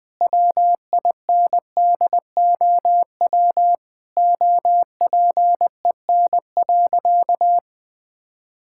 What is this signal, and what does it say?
Morse code, heard as a keyed tone, about 20 wpm, 700 Hz. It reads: WINDOW OPEN.
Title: Sentences from Top 1600 Words 20wpm